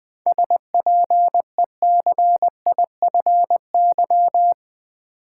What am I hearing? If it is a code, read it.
Morse code, heard as a keyed tone, about 20 wpm, 700 Hz. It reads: SPECIFY
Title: Seven Letter Words - 20wpm